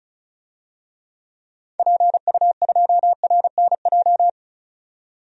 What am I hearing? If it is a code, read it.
Morse code, heard as a keyed tone, about 36 wpm, 700 Hz. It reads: PU2RNJ
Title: DXCC-100 Call Signs - 6 Characters - 35wpm